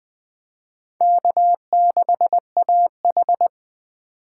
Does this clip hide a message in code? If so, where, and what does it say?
Yes: Morse code from 1.0 to 3.5 s, heard as a keyed tone, about 20 wpm, 700 Hz. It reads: K6AH